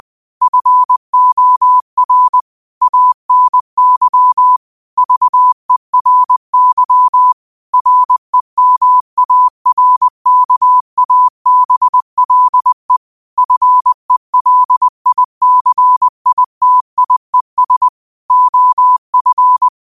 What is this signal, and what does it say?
Morse code, heard as a keyed tone, about 20 wpm, 1000 Hz. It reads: FOR ANY VERY REMARKABLE FELICITIES OF
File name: qotdfiles / 20wpm